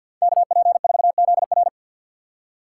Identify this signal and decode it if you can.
Morse code, heard as a keyed tone, about 50 wpm, 700 Hz. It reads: KP4ZR
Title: USA Call Signs - 5 Characters - 6x Character Spacing - 50wpm